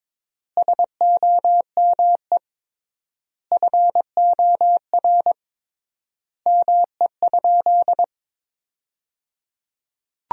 Morse code, heard as a keyed tone, about 22 wpm, 700 Hz. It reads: SOME FOR ME?
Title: 3x Word-Spacing - Sentences from Top 200 Words - 22wpm